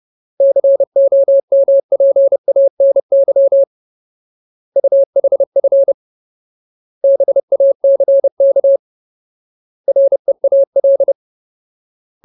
Morse code, heard as a keyed tone, about 30 wpm, 550 Hz. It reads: COMPANY UHF BACK REAL
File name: Ultimate Word List / 30WPM Ultimate